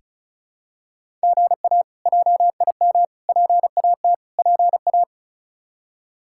Morse code, heard as a keyed tone, about 36 wpm, 700 Hz. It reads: GA JIM PAT PA
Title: K1USN SST QSO Practice - 35wpm